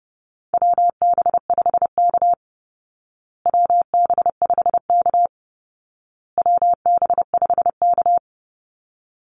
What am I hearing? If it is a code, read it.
Morse code, heard as a keyed tone, about 30 wpm, 700 Hz. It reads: WB5K WB5K WB5K
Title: Call Signs Repeated 2x - 30wpm